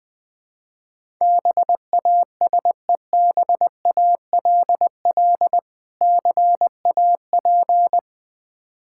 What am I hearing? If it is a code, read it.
Morse code, heard as a keyed tone, about 20 wpm, 690 Hz. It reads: BASEBALL CAP
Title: Difficult Words with Hints - 20wpm